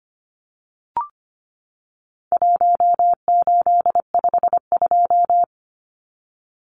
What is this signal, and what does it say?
Morse code, heard as a keyed tone, about 25 wpm, 700 Hz. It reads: 1852
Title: Numbers - 4-Digit 25wpm